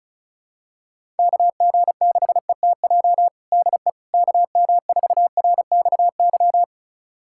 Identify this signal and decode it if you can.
Morse code, heard as a keyed tone, about 36 wpm, 700 Hz. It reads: KG6ETJ DE KM4RXY
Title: USA Callsign DE Callsign 35wpm